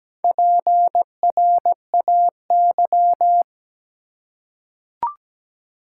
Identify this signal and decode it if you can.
Morse code, heard as a keyed tone, about 17 wpm, 700 Hz. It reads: PRAY
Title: Four Letter Words - 17wpm